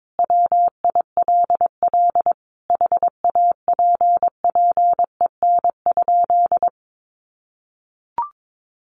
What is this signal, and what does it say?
Morse code, heard as a keyed tone, about 22 wpm, 700 Hz. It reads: WILL HAPPEN?
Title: Sentences from Top 1500 Words 22wpm